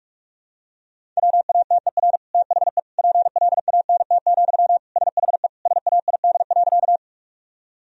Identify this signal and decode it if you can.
Morse code, heard as a keyed tone, about 46 wpm, 700 Hz. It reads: WATER THE PLANT, SHE SAID.
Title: Sentences from Top 1000 Words - Review Part 1 - 45wpm